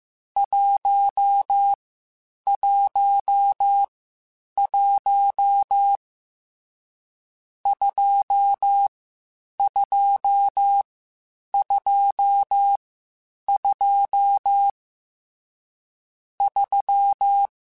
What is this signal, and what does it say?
Morse code, heard as a keyed tone, about 15 wpm, 800 Hz. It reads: 1 1 1 2 2 2 2 3